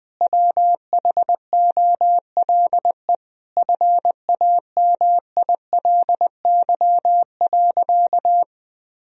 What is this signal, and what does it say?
Morse code, heard as a keyed tone, about 20 wpm, 700 Hz. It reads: WHOLE FAMILY.
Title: Sentences from Top 1000 Words - Review Part 1 - 20wpm